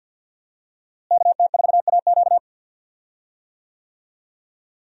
Morse code, heard as a keyed tone, about 51 wpm, 700 Hz. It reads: KT4AQ
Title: USA Call Signs - 5 Characters - Repeated 3x - 50wpm